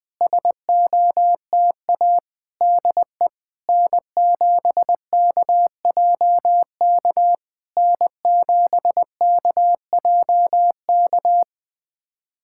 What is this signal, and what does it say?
Morse code, heard as a keyed tone, about 20 wpm, 700 Hz. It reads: SOTA DE N7KJK N7KJK
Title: SOTA QSO Practice 20wpm